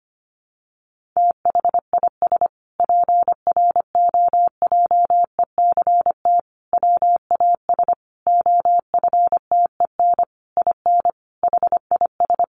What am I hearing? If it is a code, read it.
Morse code, heard as a keyed tone, about 25 wpm, 700 Hz. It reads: THIS PROJECT WAS OFTEN IN HIS
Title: Sentences from Top 1400 Words 25wpm